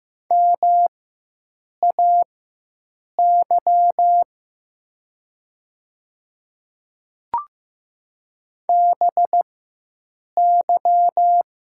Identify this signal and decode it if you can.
Morse code, heard as a keyed tone, about 15 wpm, spaced as for 7 wpm, 700 Hz. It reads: MAY BY
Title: 4x Character Spacing - Top 200 Words - 15wpm